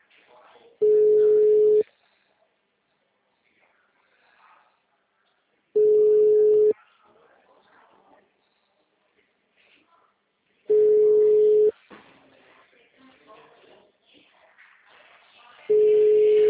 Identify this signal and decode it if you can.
Special information tone, with speech.
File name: out- -2004- - - .14167